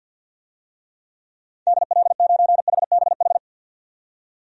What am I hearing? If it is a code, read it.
Morse code, heard as a keyed tone, about 50 wpm, 700 Hz. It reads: DL9HBH
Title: DXCC-100 Call Signs - 6 Characters - 50wpm